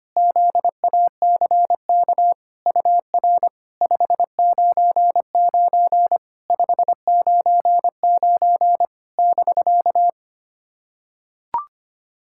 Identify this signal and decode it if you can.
Morse code, heard as a keyed tone, about 25 wpm, 700 Hz. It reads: ZACK UR 599 599 <BK>
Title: SOTA QSO Practice 25wpm